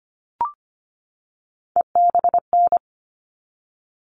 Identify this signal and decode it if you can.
Morse code, heard as a keyed tone, about 25 wpm, 700 Hz. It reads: EBN
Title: All Cut Numbers - 3-Digit 25wpm